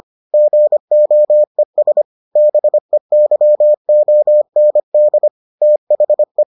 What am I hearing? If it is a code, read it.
Morse code, heard as a keyed tone, about 25 wpm, 600 Hz. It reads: GOES BEYOND THE